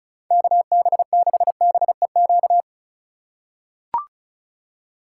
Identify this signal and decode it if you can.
Morse code, heard as a keyed tone, about 36 wpm, 700 Hz. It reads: KB6BEQ
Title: USA Call Signs - 6 Characters - 2x Character Spacing - 35wpm